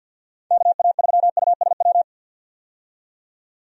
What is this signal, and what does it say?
Morse code, heard as a keyed tone, about 49 wpm, 700 Hz. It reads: KA3USW
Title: USA Call Signs - 6 Characters - 50wpm